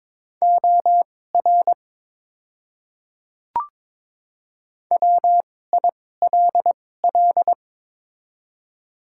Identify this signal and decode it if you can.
Morse code, heard as a keyed tone, about 22 wpm, spaced as for 16 wpm, 700 Hz. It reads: OR WILL
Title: 2x Character Spacing - Top 500 Words - 22wpm